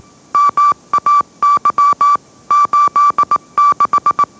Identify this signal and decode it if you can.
Morse code, heard as a keyed tone, about 18 wpm, 1160 Hz. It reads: MAY 86